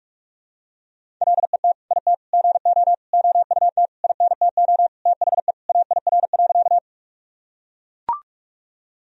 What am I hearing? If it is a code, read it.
Morse code, heard as a keyed tone, about 44 wpm, 700 Hz. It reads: LET IT GO OUT INTO THE AIR.
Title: Sentences from Top 1000 Words - Review Part 1 - 45wpm